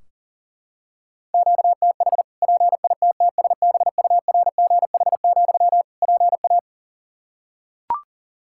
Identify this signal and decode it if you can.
Morse code, heard as a keyed tone, about 40 wpm, 700 Hz. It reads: QTH PITTSBURGH, PA
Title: Practice QSO - Exchange 1 - Beginning - 40wpm